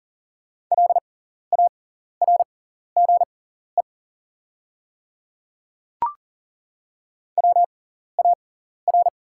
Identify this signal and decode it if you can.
Morse code, heard as a keyed tone, about 40 wpm, spaced as for 14 wpm, 700 Hz. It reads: LARGE WAR
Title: 6x Character Spacing - Top 500 Words - 40wpm